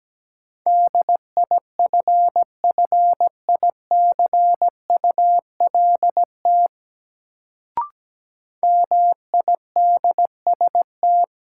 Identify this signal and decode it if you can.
Morse code, heard as a keyed tone, about 17 wpm, 700 Hz. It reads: DIFFICULT MIDST
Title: Sets of 4 Words - New Word Drill - 17wpm